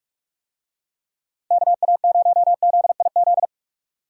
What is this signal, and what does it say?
Morse code, heard as a keyed tone, about 42 wpm, 700 Hz. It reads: KA0ZIZ
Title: USA Call Signs - 6 Characters - Repeated 3x - 45wpm